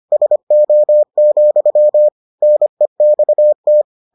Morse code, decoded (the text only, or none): SO, NEXT